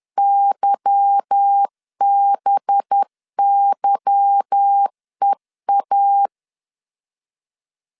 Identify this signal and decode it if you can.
Morse code, heard as a keyed tone, about 10 wpm, 800 Hz. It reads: YBYEA
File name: LEZIONI / 8 YJ45 / 8 Y 10wpm 2615